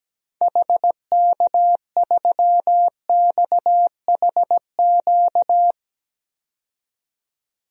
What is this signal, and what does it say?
Morse code, heard as a keyed tone, about 17 wpm, 700 Hz. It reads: HK3XHQ
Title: DXCC-100 Call Signs - 6 Characters - 17wpm